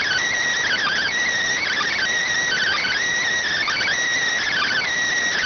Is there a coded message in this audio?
Yes: Emergency Alert System (SAME) tones (header bursts) from 0.6 to 4.9 s, heard as harsh buzzing bursts.